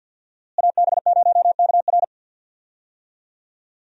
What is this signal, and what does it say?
Morse code, heard as a keyed tone, about 47 wpm, 700 Hz. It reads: AB0KR
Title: USA Call Signs - 5 Characters - Repeated 2x - 50wpm